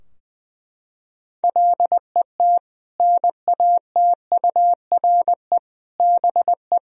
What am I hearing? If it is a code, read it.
Morse code, heard as a keyed tone, about 20 wpm, 700 Hz. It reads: LET NATURE BE
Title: Sentences from Top 1000 Words - Review Part 3 - 20wpm